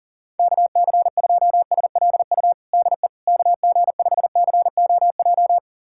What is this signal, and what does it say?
Morse code, heard as a keyed tone, about 40 wpm, 700 Hz. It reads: KC2SLU DE KG5COJ
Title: USA Callsign DE Callsign - 6 Characters - 40wpm